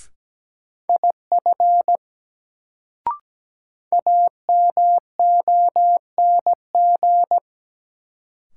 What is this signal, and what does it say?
Morse code, heard as a keyed tone, about 17 wpm, 700 Hz. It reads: IF AMONG
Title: Sets of 2 Words - New Word Drill - 17wpm